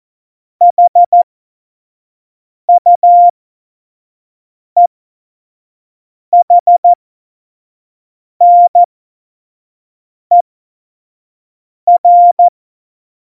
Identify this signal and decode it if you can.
Morse code, heard as a keyed tone, about 14 wpm, spaced as for 5 wpm, 700 Hz. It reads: HUEHNER
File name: Morsekurs / Witze / Witz 08. Mai 2018, 25 F70